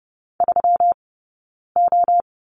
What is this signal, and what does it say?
Morse code, heard as a keyed tone, about 30 wpm, 700 Hz. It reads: 3 O
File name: Lesson 20d - Number 3 - Two Characters 30wpm